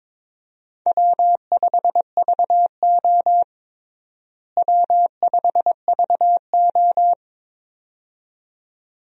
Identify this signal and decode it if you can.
Morse code, heard as a keyed tone, about 22 wpm, 700 Hz. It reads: W5VO W5VO
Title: USA Call Signs - 4 Characters - Repeated 2x - 22wpm